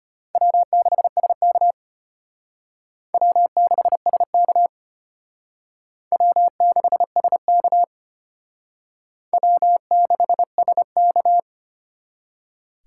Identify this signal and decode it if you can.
Morse code, heard as a keyed tone, about 32 wpm, 700 Hz. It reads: W6SK W6SK W6SK W6SK